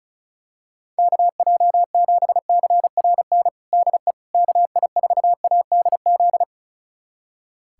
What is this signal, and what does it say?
Morse code, heard as a keyed tone, about 34 wpm, 700 Hz. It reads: KJ7CRN DE KI4ADZ
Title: USA Callsign DE Callsign - 6 Characters - 35wpm